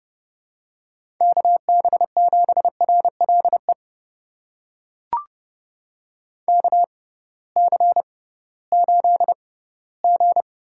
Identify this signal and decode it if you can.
Morse code, heard as a keyed tone, about 30 wpm, spaced as for 17 wpm, 700 Hz. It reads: KB7RLE KC8G
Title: USA Call Signs - 6 Characters - 6x Character Spacing - 30wpm